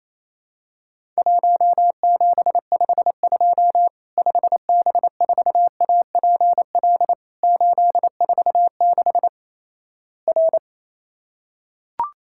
Morse code, heard as a keyed tone, about 28 wpm, 700 Hz. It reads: 1752 5B4APL 846 R
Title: Worked All Europe DX Contest - QTC Exchange - 28wpm